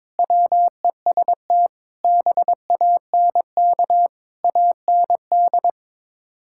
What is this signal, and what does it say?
Morse code, heard as a keyed tone, about 22 wpm, 700 Hz. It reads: WEST BANK AND
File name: practice / 22wpm / Build A Phrase - 5-Word Phrases - 22wpm